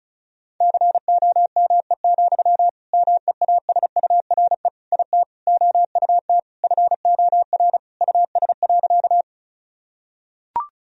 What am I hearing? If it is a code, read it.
Morse code, heard as a keyed tone, about 36 wpm, 700 Hz. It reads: COME, MEASURE IT OUT FOR US.
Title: Sentences from Top 1000 Words - Review Part 2 - 35wpm